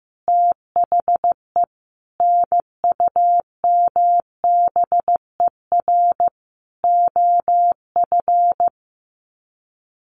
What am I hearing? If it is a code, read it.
Morse code, heard as a keyed tone, about 15 wpm, 700 Hz. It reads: THE NUMBER OF
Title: Sets of 5 Words 15wpm